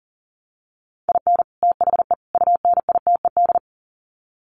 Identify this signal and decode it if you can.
Morse code, heard as a keyed tone, about 40 wpm, 700 Hz. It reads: IN THE UNITED
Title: Sets of 3 Words 40wpm